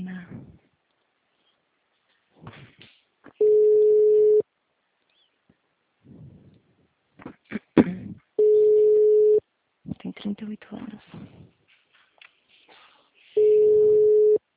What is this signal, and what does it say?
Ringback tone (Europe, 425 Hz), with speech.
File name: out- -2018- - - .52463